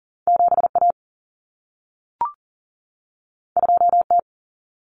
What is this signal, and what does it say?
Morse code, heard as a keyed tone, about 40 wpm, 700 Hz. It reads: 7A 2T
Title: Common Cut Numbers - 2-Digit 40wpm